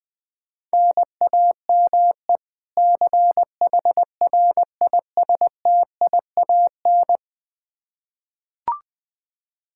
Morse code, heard as a keyed tone, about 21 wpm, 700 Hz. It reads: NAME CHRISTIAN
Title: K1USN SST QSO Practice 20wpm